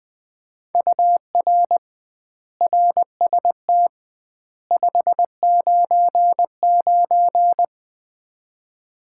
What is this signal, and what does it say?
Morse code, heard as a keyed tone, about 20 wpm, 700 Hz. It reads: UR RST 599
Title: Common QSO Phrases - 2x Word-Spacing - 20wpm